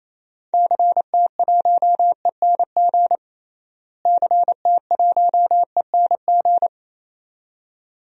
Morse code, heard as a keyed tone, about 28 wpm, 700 Hz. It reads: CT1ENG CT1ENG
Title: DXCC-100 Call Signs - 6 Characters - Repeated 2x - 28wpm